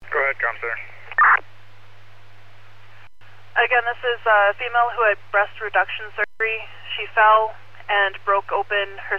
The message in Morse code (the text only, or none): none